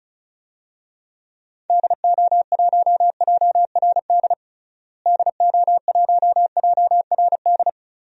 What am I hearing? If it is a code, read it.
Morse code, heard as a keyed tone, about 34 wpm, 700 Hz. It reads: DO1JRD DO1JRD
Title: DXCC-100 Call Signs - 6 Characters - Repeated 3x - 35wpm